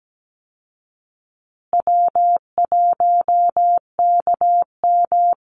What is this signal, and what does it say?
Morse code, heard as a keyed tone, about 17 wpm, 700 Hz. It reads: W1KM
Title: Call Signs Repeated 2x - 17wpm